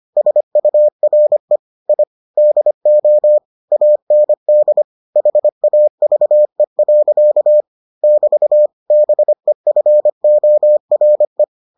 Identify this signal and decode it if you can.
Morse code, heard as a keyed tone, about 25 wpm, 600 Hz. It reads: SURE I DO AND HAVE. <BT> BEFORE